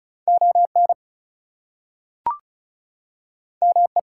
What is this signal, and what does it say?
Morse code, heard as a keyed tone, about 36 wpm, 700 Hz. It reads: ON ME
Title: Two Letter Words - 35wpm